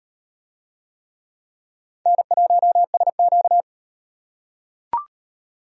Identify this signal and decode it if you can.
Morse code, heard as a keyed tone, about 35 wpm, 700 Hz. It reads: N1SQ